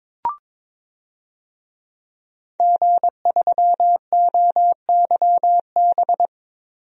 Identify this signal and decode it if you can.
Morse code, heard as a keyed tone, about 22 wpm, 700 Hz. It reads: G3OYB
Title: DXCC-100 Call Signs - 5 Characters - Repeated 2x - 22wpm